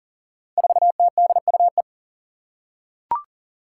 Morse code, heard as a keyed tone, about 38 wpm, 700 Hz. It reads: 4TDUE
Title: Five Challenging Letter-Number Combinations 40wpm